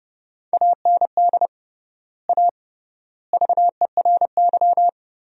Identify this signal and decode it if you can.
Morse code, heard as a keyed tone, about 30 wpm, 700 Hz. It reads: AND A VERY
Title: 3x Word-Spacing - Sentences from Top 200 Words - 30wpm